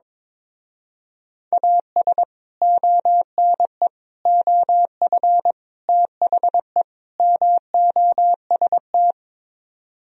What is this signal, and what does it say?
Morse code, heard as a keyed tone, about 23 wpm, 700 Hz. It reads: AS ONE OF THE MOST
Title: Speed-Racing - Sets of 5 Words 22wpm